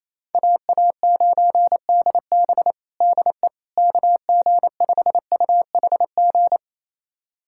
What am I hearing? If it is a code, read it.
Morse code, heard as a keyed tone, about 28 wpm, 700 Hz. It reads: AA9DB DE KG5UHG
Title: USA Callsign DE Callsign 28wpm